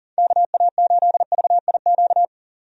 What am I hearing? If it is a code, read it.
Morse code, heard as a keyed tone, about 40 wpm, 700 Hz. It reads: KA8VIQ
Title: USA Call Signs - 6 Characters - 40wpm